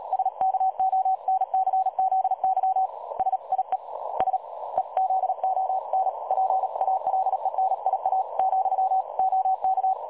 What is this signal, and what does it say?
Morse code, heard as a keyed tone, about 38 wpm, spaced as for 33 wpm, 760 Hz. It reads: SKONCZY SIE SEZON PALENIA , OK
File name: dlazaawansowanych / nagraniazpasma / websdr recording 3565.0kHz